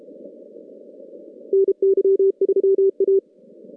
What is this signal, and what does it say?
Morse code, heard as a keyed tone, about 30 wpm, 390 Hz. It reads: NY3A
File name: contests / CW-SS-k5tr- -2200